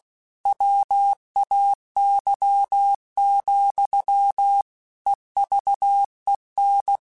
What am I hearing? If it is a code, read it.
Morse code, heard as a keyed tone, about 16 wpm, 800 Hz. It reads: WAY, EVEN
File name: Technical Subjects / Technical Subjects-mp3s / OR Gate @15